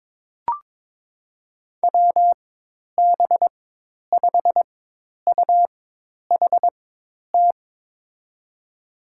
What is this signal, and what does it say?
Morse code, heard as a keyed tone, about 22 wpm, spaced as for 10 wpm, 700 Hz. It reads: WB5UHT
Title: USA Call Signs - 6 Characters - 4x Character Spacing - 22wpm